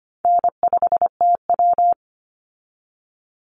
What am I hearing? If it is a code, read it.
Morse code, heard as a keyed tone, about 25 wpm, 700 Hz. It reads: N5TW
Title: Easy Call Signs - 25wpm